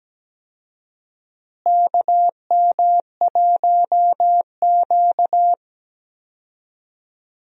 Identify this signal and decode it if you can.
Morse code, heard as a keyed tone, about 17 wpm, 700 Hz. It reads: KM1Q
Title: USA Call Signs - 4 Characters - Repeated 3x - 17wpm